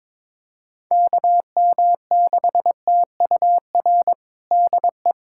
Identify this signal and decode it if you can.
Morse code, heard as a keyed tone, about 21 wpm, 700 Hz. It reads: KM6TUR DE
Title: USA Callsign DE Callsign 22wpm